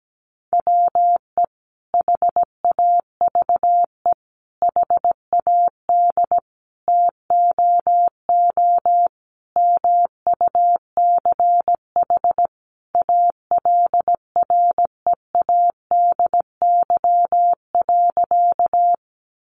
Morse code, heard as a keyed tone, about 17 wpm, 700 Hz. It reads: WE HAVE HAD TOO MUCH ALREADY.
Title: Sentences from Top 700 Words 17wpm